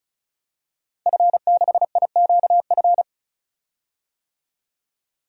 Morse code, heard as a keyed tone, about 34 wpm, 700 Hz. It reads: F6IQF
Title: DXCC-100 Call Signs - 5 Characters - Repeated 2x - 35wpm